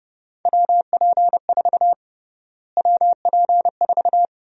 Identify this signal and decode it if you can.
Morse code, heard as a keyed tone, about 30 wpm, 700 Hz. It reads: WP4 WP4
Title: USA Call Sign Prefixes - Repeated 2x - 30wpm